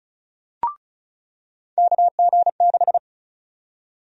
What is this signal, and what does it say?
Morse code, heard as a keyed tone, about 34 wpm, 700 Hz. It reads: KG6
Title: USA Call Sign Prefixes - 35wpm